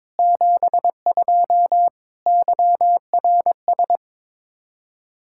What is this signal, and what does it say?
Morse code, heard as a keyed tone, about 22 wpm, 700 Hz. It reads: 72 YRS
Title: Practice QSO - Exchange 3 - More Info - 22wpm